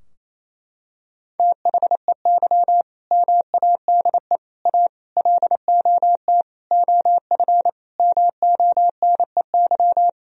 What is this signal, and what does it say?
Morse code, heard as a keyed tone, about 28 wpm, 700 Hz. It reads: THEY MADE A LOT OF MONEY
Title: Sentences from Top 1000 Words - Review Part 4 - 28wpm